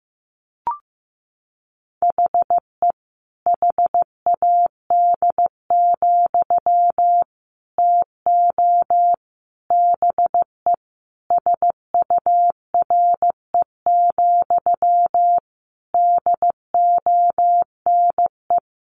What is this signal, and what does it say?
Morse code, heard as a keyed tone, about 15 wpm, 700 Hz. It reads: HE HAD, TO BE SURE, DONE
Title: Sentences from Top 500 Words 15wpm